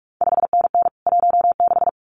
Morse code, heard as a keyed tone, about 46 wpm, 700 Hz. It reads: 5NN 16
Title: CQ WW DX Contest Exchange - 45wpm